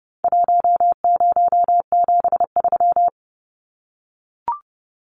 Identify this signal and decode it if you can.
Morse code, heard as a keyed tone, about 30 wpm, 700 Hz. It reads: 1073
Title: Numbers - 4-Digit 30wpm